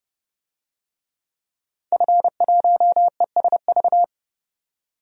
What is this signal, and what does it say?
Morse code, heard as a keyed tone, about 30 wpm, 700 Hz. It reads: F1ESV
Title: DXCC-100 Call Signs - 5 Characters - 30wpm